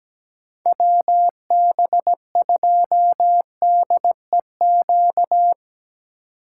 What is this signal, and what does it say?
Morse code, heard as a keyed tone, about 19 wpm, 700 Hz. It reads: WB2DEQ
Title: Speed Racing - USA 2x3 Call Signs 17wpm